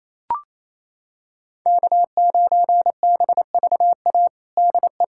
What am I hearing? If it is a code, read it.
Morse code, heard as a keyed tone, about 28 wpm, 700 Hz. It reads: K9BVA DE
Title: USA Callsign DE Callsign 28wpm